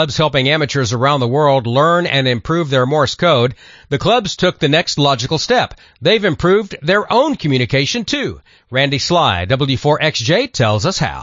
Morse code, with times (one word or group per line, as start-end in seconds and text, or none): none